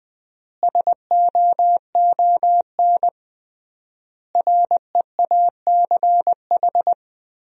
Morse code, heard as a keyed tone, about 20 wpm, 700 Hz. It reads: SOON REACH
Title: 3x Word-Spacing - Sentences from Top 400 Words - 20wpm